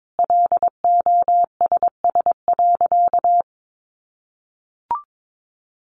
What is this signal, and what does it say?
Morse code, heard as a keyed tone, about 22 wpm, 700 Hz. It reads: LOSS.
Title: Sentences from Top 1400 Words 22wpm